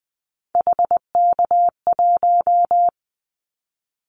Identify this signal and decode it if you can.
Morse code, heard as a keyed tone, about 20 wpm, 700 Hz. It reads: HK1
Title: Callsign Prefixes for Top 100 DXCC - 20wpm